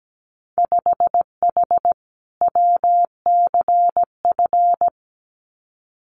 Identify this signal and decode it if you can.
Morse code, heard as a keyed tone, about 17 wpm, 700 Hz. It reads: 5H WCF
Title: Winter Field Day Exchanges Encore 17wpm